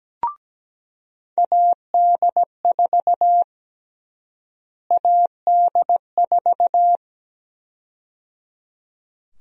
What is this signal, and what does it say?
Morse code, heard as a keyed tone, about 17 wpm, 700 Hz. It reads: AD4 AD4
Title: USA Call Sign Prefixes - Repeated 2x - 17wpm